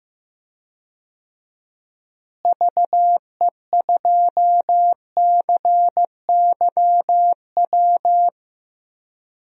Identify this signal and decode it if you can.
Morse code, heard as a keyed tone, about 15 wpm, 700 Hz. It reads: VE2CYW
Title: DXCC-100 Call Signs - 6 Characters - Repeated 3x - 15wpm